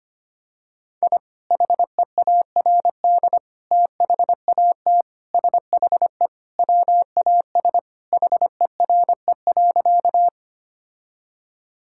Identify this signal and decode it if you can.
Morse code, heard as a keyed tone, about 25 wpm, 700 Hz. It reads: I HEARD THAT SHE WAS HERE.
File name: practice / 25wpm / Sentences with 3-Word Phrases - 25wpm